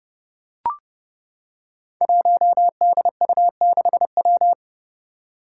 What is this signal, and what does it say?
Morse code, heard as a keyed tone, about 29 wpm, 700 Hz. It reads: 1DU6W
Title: Five Challenging Letter-Number Combinations 30wpm